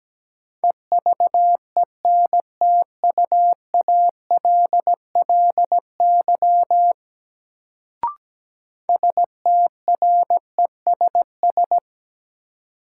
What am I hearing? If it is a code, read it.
Morse code, heard as a keyed tone, about 17 wpm, 700 Hz. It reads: EVENTUALLY STRESS
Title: Sentences from Top 1000 Words - Review Part 6 - New Word Drill - 17wpm